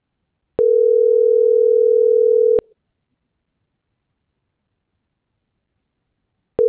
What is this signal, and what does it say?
Ringback tone (North America), 440 + 480 Hz.